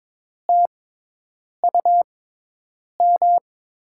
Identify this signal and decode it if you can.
Morse code, heard as a keyed tone, about 22 wpm, spaced as for 8 wpm, 700 Hz. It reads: TUM
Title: USA Call Signs - 5 Characters - 6x Character Spacing - 22wpm